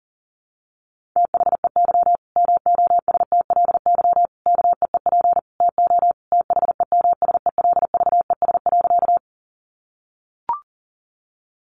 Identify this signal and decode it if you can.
Morse code, heard as a keyed tone, about 40 wpm, 700 Hz. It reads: THEY MOSTLY KEEP TO THEMSELVES.
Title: Sentences from Top 1300 Words 40wpm